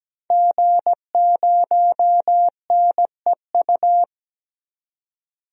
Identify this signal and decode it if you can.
Morse code, heard as a keyed tone, about 17 wpm, 700 Hz. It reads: G0NEU
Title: DXCC-100 Call Signs - 5 Characters - 17wpm